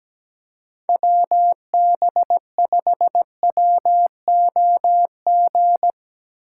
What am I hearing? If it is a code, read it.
Morse code, heard as a keyed tone, about 18 wpm, 700 Hz. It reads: WB5WOG
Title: Speed Racing - USA 2x3 Call Signs 17wpm